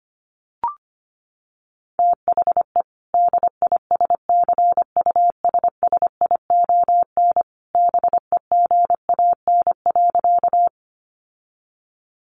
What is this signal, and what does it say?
Morse code, heard as a keyed tone, about 25 wpm, 700 Hz. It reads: THE DISCUSSION BEGAN.
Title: Sentences from Top 1100 Words 25wpm